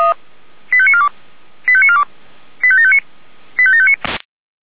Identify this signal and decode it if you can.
Five-tone selective call (ZVEI), heard as quick runs of five tones, tones of about 70 ms. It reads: 85942 85942 85759 85759